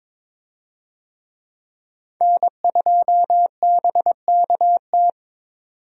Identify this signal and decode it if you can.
Morse code, heard as a keyed tone, about 22 wpm, 700 Hz. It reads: N2BKT
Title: USA Call Signs - 5 Characters - 22wpm